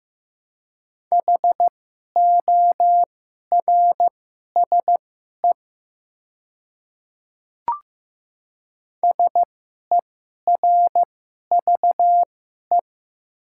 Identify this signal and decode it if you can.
Morse code, heard as a keyed tone, about 15 wpm, spaced as for 11 wpm, 700 Hz. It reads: HORSE SERVE
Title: 2x Character Spacing - Top 500 Words - 15wpm